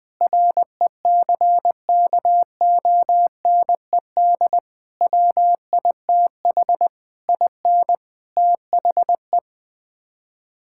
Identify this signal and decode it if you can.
Morse code, heard as a keyed tone, about 20 wpm, 700 Hz. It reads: RECKONED WITH IN THE
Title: Sets of 8 Words - 20wpm